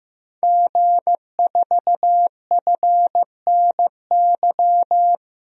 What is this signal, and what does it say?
Morse code, heard as a keyed tone, about 15 wpm, 700 Hz. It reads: G4FNY
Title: DXCC-100 Call Signs - 5 Characters - 15wpm